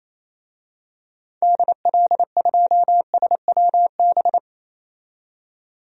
Morse code, heard as a keyed tone, about 28 wpm, 700 Hz. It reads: DL2SWB
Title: DXCC-100 Call Signs - 6 Characters - 28wpm